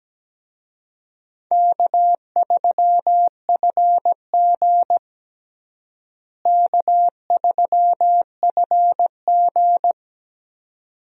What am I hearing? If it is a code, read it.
Morse code, heard as a keyed tone, about 17 wpm, 700 Hz. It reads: K3FG K3FG
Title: USA Call Signs - 4 Characters - Repeated 2x - 17wpm